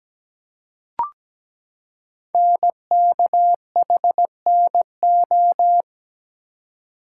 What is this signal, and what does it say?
Morse code, heard as a keyed tone, about 17 wpm, 700 Hz. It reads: NKHNO
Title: Code Groups - 5 Letters - 17wpm